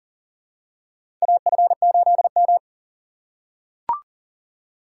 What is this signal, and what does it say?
Morse code, heard as a keyed tone, about 40 wpm, 700 Hz. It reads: AF8M
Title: USA Call Signs - 4 Characters - 3x Character Spacing - 40wpm